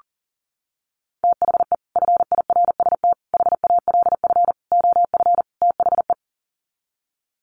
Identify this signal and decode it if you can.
Morse code, heard as a keyed tone, about 40 wpm, 700 Hz. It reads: THE FIRST HALF OF THE
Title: Sets of 5 Words 40wpm